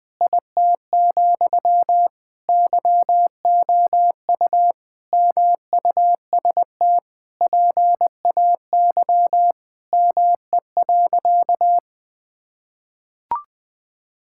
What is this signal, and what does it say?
Morse code, heard as a keyed tone, about 21 wpm, 700 Hz. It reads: IT, YOU MUST PAY ME.
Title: Sentences from Top 1000 Words - Review Part 5 - 20wpm